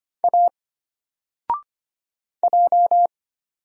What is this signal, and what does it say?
Morse code, heard as a keyed tone, about 25 wpm, 700 Hz. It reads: A J
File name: Single-Letter-Number-25wpm